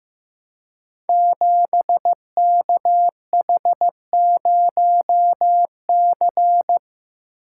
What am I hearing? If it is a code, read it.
Morse code, heard as a keyed tone, about 15 wpm, 700 Hz. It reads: 7KH0C
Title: Code Groups - 5 Letter:Numbers - 15wpm